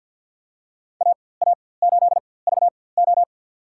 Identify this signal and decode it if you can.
Morse code, heard as a keyed tone, about 49 wpm, spaced as for 23 wpm, 700 Hz. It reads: AA8VO
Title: USA Call Signs - 5 Characters - 4x Character Spacing - 50wpm